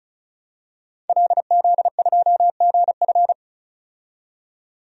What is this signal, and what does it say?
Morse code, heard as a keyed tone, about 36 wpm, 700 Hz. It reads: LZ2GF
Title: DXCC-100 Call Signs - 5 Characters - 35wpm